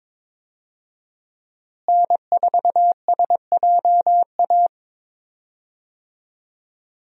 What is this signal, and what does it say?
Morse code, heard as a keyed tone, about 21 wpm, 700 Hz. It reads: N4SJA
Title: USA Call Signs - 5 Characters - Repeated 2x - 22wpm